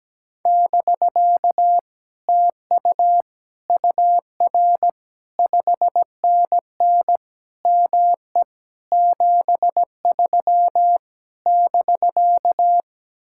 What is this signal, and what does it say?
Morse code, heard as a keyed tone, about 17 wpm, 700 Hz. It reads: <BK> TU UR 5NN ME 73 <BK>
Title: Speed-Racing Common QSO Phrases - 17wpm